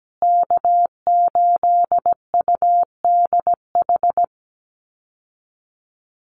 Morse code, heard as a keyed tone, about 17 wpm, 700 Hz. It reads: K8UDH
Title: Call Signs Repeated 3x - 17wpm